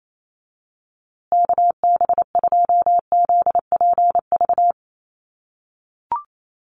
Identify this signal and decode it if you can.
Morse code, heard as a keyed tone, about 28 wpm, 700 Hz. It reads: KB2ZPV
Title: USA Call Signs Repeated 2x 28wpm